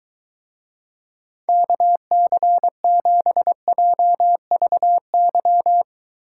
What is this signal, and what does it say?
Morse code, heard as a keyed tone, about 23 wpm, 700 Hz. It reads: KC7JVY